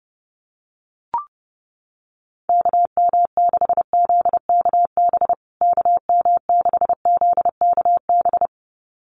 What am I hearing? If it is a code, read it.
Morse code, heard as a keyed tone, about 30 wpm, 700 Hz. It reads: KM6ZKB KM6ZKB
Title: USA Call Signs Repeated 2x 30wpm